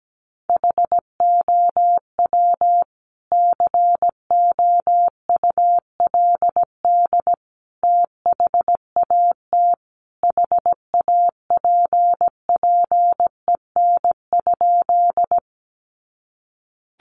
Morse code, heard as a keyed tone, about 17 wpm, 700 Hz. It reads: HOW COULD THAT HAPPEN?
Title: Sentences from Top 400 Words 17wpm